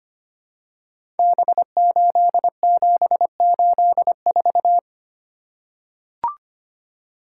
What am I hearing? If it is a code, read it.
Morse code, heard as a keyed tone, about 25 wpm, 700 Hz. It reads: B8784
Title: Code Groups - 5 Letter:Numbers - 25wpm